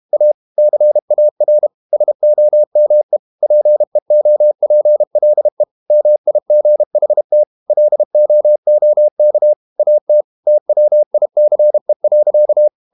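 Morse code, heard as a keyed tone, about 32 wpm, 600 Hz. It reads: A CAR SOME PEOPLE MIGHT LOOK AT TWICE.
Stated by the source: Morse code practice at 32 wpm from CW med Gnister, reading Ken Follett's novel Never in English.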